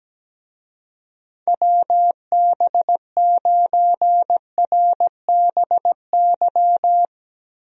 Morse code, heard as a keyed tone, about 17 wpm, 700 Hz. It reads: WB9RBY
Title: USA Call Signs - 6 Characters - 17wpm